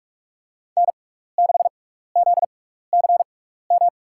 Morse code, heard as a keyed tone, about 42 wpm, spaced as for 15 wpm, 700 Hz. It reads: N6ZCM
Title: USA Call Signs - 5 Characters - 6x Character Spacing - 45wpm